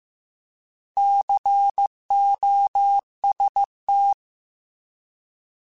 Morse code, heard as a keyed tone, about 15 wpm, 800 Hz. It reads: COST
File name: Alphabetical / Alphabetical mp3s / Cs @13